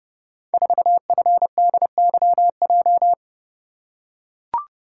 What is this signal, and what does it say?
Morse code, heard as a keyed tone, about 29 wpm, 700 Hz. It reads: 4FDYJ
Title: Code Groups - 5 Letter:Numbers - 30wpm